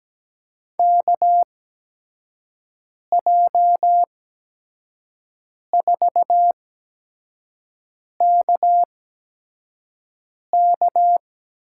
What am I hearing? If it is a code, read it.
Morse code, heard as a keyed tone, about 17 wpm, spaced as for 5 wpm, 700 Hz. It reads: KJ4KK
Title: USA Call Signs - 6 Characters - 8x Character Spacing - 17wpm